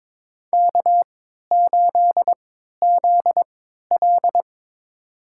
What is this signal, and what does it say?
Morse code, heard as a keyed tone, about 22 wpm, spaced as for 13 wpm, 700 Hz. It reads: K8ZL
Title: USA Call Signs - 4 Characters - 3x Character Spacing - 22wpm